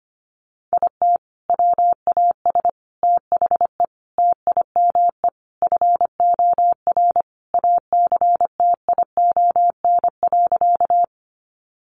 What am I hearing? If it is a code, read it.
Morse code, heard as a keyed tone, about 25 wpm, 700 Hz. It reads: IT WAS THE TIME FOR ACTION.
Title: Sentences from Top 700 Words 25wpm